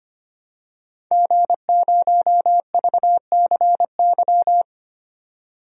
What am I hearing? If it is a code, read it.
Morse code, heard as a keyed tone, about 25 wpm, 700 Hz. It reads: G0VCY
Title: DXCC-100 Call Signs - 5 Characters - 25wpm